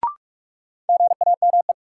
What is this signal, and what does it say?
Morse code, heard as a keyed tone, about 42 wpm, 700 Hz. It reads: GAME
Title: Sentences from Top 500 Words - New Word Drill - 45wpm